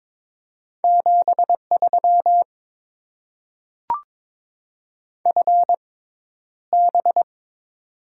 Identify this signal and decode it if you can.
Morse code, heard as a keyed tone, about 22 wpm, spaced as for 11 wpm, 700 Hz. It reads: 73 FB
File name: Top 20 QSO Elements - 6x Character Spacing - 22wpm